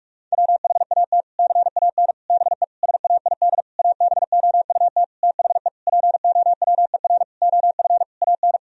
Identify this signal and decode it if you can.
Morse code, heard as a keyed tone, about 46 wpm, 700 Hz. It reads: WHAT CAN BE SAID ABOUT THE POWER OF AN
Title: Sentences from Top 1000 Words - Review Part 4 - 45wpm